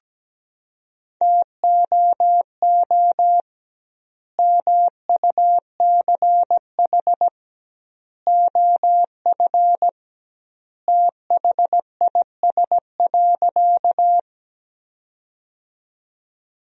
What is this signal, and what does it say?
Morse code, heard as a keyed tone, about 17 wpm, 700 Hz. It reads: TOO MUCH OF THIS.
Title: 2x Word-Spacing - Sentences from Top 200 Words - 17wpm